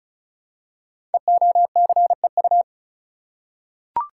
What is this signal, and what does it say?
Morse code, heard as a keyed tone, about 36 wpm, 700 Hz. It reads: EOCEU
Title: Code Groups - 5 Letter:Numbers - 35wpm